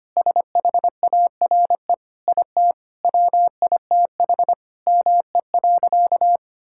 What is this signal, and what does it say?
Morse code, heard as a keyed tone, about 25 wpm, 700 Hz. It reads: SHARE IT WITH ME.
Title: Sentences from Top 1000 Words - Review Part 6 - 25wpm